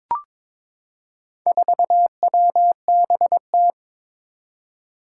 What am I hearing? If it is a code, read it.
Morse code, heard as a keyed tone, about 22 wpm, 700 Hz. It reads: 4WBT
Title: Four Challenging Letter-Number Combinations 22wpm